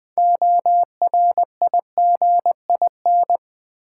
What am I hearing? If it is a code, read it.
Morse code, heard as a keyed tone, about 20 wpm, 700 Hz. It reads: ORIGIN
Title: Build A Word - 3 Letters At A Time - 6-Letter Words - 20wpm